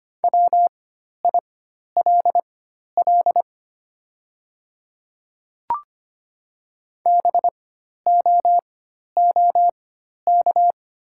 Morse code, heard as a keyed tone, about 25 wpm, spaced as for 12 wpm, 700 Hz. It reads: WILL BOOK
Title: 4x Character Spacing - Top 300 Words - 25wpm